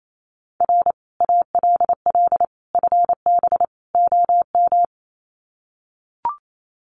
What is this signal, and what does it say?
Morse code, heard as a keyed tone, about 28 wpm, 700 Hz. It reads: R ALL FB OM
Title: Common QSO Phrases 28wpm